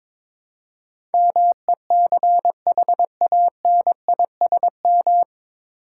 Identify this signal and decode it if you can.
Morse code, heard as a keyed tone, about 22 wpm, 700 Hz. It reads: MECHANISM
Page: Morse Code Ninja practice 22wpm